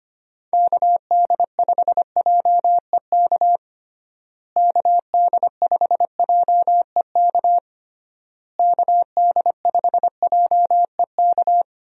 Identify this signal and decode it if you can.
Morse code, heard as a keyed tone, about 25 wpm, 700 Hz. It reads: KD5JEK KD5JEK KD5JEK
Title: USA Call Signs - 6 Characters - Repeated 3x - 25wpm